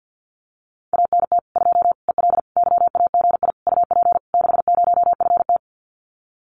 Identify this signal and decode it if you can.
Morse code, heard as a keyed tone, about 50 wpm, 700 Hz. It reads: ANT 2 EL YAGI UP 60FT
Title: Common QSO Phrases Encore 50wpm